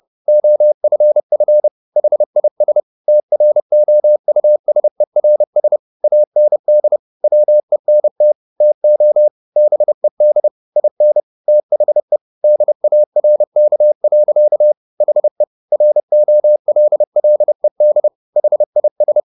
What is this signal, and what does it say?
Morse code, heard as a keyed tone, about 30 wpm, 600 Hz. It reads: OFF HIS TROUSERS AND WENT TO BED IN THE DARK. HE ROLLED HIS
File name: the old man and the sea / the old man and the sea 1 30wpm